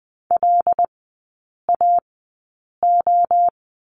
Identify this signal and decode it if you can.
Morse code, heard as a keyed tone, about 20 wpm, spaced as for 8 wpm, 700 Hz. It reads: LAO
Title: 074 Lesson 13e - Letter L - Three Characters 20wpm